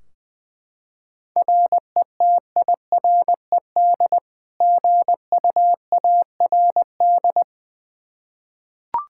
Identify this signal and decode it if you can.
Morse code, heard as a keyed tone, about 20 wpm, 700 Hz. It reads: RETIRED GUARD
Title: K1USN SST QSO Practice 20wpm